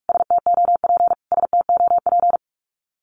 Morse code, heard as a keyed tone, about 44 wpm, 700 Hz. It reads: STOP STOP
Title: Top 500 Words Repeated 45wpm